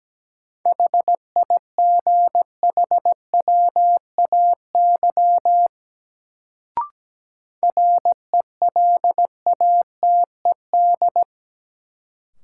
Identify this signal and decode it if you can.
Morse code, heard as a keyed tone, about 17 wpm, 700 Hz. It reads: HIGHWAY RELATED
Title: Sentences from Top 1400 Words - New Word Drill Alt - 17wpm